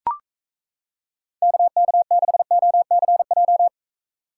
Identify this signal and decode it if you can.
Morse code, heard as a keyed tone, about 36 wpm, 700 Hz. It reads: KK6OCJ